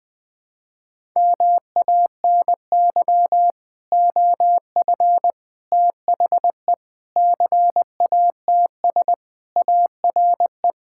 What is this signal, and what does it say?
Morse code, heard as a keyed tone, about 20 wpm, 700 Hz. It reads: MANY OF THE CATS ARE
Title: Sentences with 3-Word Phrases - Spoken Beginning - 20wpm